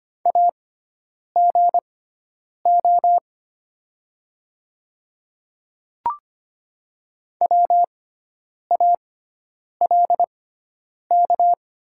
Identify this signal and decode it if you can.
Morse code, heard as a keyed tone, about 25 wpm, spaced as for 9 wpm, 700 Hz. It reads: AGO WALK
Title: 6x Character Spacing - Top 500 Words - 25wpm